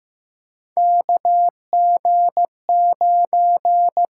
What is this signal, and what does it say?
Morse code, heard as a keyed tone, about 15 wpm, 700 Hz. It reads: KG9
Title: USA Call Signs - 5 Characters - 8x Character Spacing - 15wpm